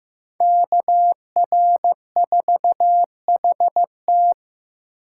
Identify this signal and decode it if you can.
Morse code, heard as a keyed tone, about 18 wpm, 700 Hz. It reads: KR4HT
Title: Speed Racing - USA 2x2 Call Signs 15wpm